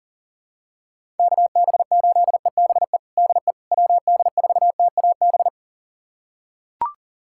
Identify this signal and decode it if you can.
Morse code, heard as a keyed tone, about 40 wpm, 700 Hz. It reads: KB8EBE DE WD4TAB
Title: USA Callsign DE Callsign - 6 Characters - 40wpm